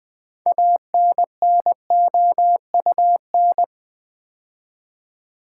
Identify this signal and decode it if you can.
Morse code, heard as a keyed tone, about 20 wpm, 700 Hz. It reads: ANNOUN
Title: Build A Word - 2 Letters At A Time - 8-Letter Words - 20wpm